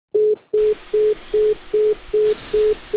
Congestion tone (425 Hz).